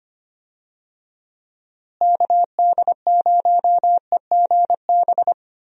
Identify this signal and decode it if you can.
Morse code, heard as a keyed tone, about 25 wpm, 700 Hz. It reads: KD0EGB
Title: USA Call Signs - 6 Characters - Repeated 2x - 25wpm